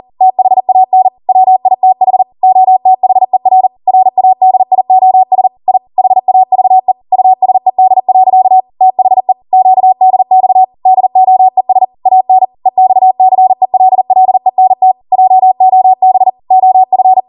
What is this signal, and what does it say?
Morse code, heard as a keyed tone, about 40 wpm, 750 Hz. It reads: THAN WITH OTHER RADIOS I HAVE USED. THE QDX DOES AN EXCELLENT JOB OF